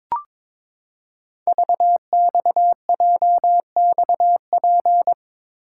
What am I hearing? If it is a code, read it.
Morse code, heard as a keyed tone, about 22 wpm, 700 Hz. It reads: VXJXP
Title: Code Groups - 5 Letter:Numbers - 22wpm